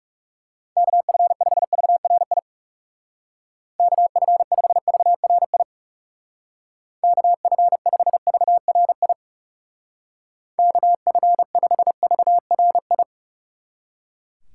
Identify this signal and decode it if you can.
Morse code, heard as a keyed tone, about 39 wpm, 700 Hz. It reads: KF5VRI KF5VRI KF5VRI KF5VRI